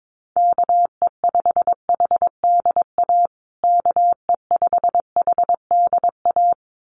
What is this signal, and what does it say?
Morse code, heard as a keyed tone, about 22 wpm, 700 Hz. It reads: KE5HDA KE5HDA
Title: USA Call Signs Repeated 2x 22wpm